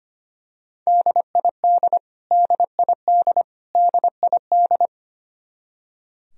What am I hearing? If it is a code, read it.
Morse code, heard as a keyed tone, about 25 wpm, 700 Hz. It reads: DID DID DID
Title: Top 100 Words Repeated 25wpm